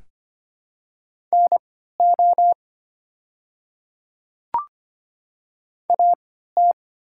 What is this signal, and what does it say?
Morse code, heard as a keyed tone, about 25 wpm, spaced as for 14 wpm, 700 Hz. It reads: NO AT
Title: 3x Character Spacing - Top 100 Words - 25wpm